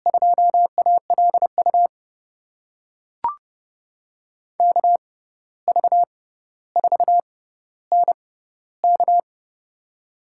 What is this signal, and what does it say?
Morse code, heard as a keyed tone, about 30 wpm, spaced as for 14 wpm, 700 Hz. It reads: 2ALU KV4NK
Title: USA Call Signs - 5 Characters - 6x Character Spacing - 30wpm